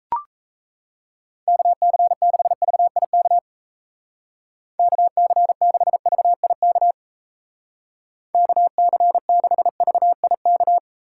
Morse code, heard as a keyed tone, about 37 wpm, 700 Hz. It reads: KC6VIK KC6VIK KC6VIK